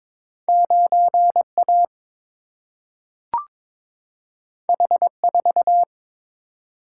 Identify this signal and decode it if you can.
Morse code, heard as a keyed tone, about 22 wpm, 700 Hz. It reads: 9A H4
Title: Single-Letter Single-Number 22wpm